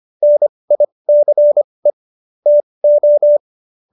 Morse code, heard as a keyed tone, about 25 wpm, spaced as for 20 wpm, 590 Hz. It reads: NICE TO
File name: qso207b 20